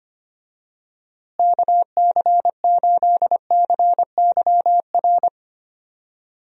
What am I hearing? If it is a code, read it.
Morse code, heard as a keyed tone, about 25 wpm, 700 Hz. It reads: KC8CYR
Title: USA Call Signs - 6 Characters - Repeated 3x - 25wpm